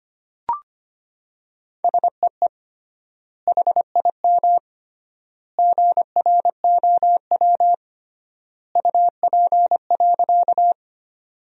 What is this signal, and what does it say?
Morse code, heard as a keyed tone, about 25 wpm, 700 Hz. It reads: SEE HIM GROW UP.
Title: 3x Word-Spacing - Sentences from Top 300 Words - 25wpm